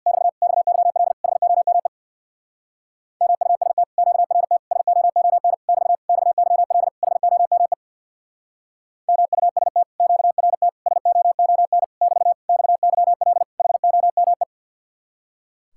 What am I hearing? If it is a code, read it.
Morse code, heard as a keyed tone, about 66 wpm, 700 Hz. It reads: <BT> XYL HOME MUST QRT SOON <BT> XYL HOME MUST QRT SOON <BT> XYL HOME